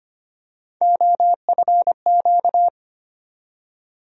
Morse code, heard as a keyed tone, about 25 wpm, 700 Hz. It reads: OFQ
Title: USA Call Sign Postfixes - 3 Letters 25wpm